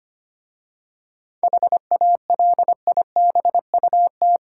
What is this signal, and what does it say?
Morse code, heard as a keyed tone, about 25 wpm, 700 Hz. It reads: HALIBUT
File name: practice / 25wpm / Difficult Words with Hints - 25wpm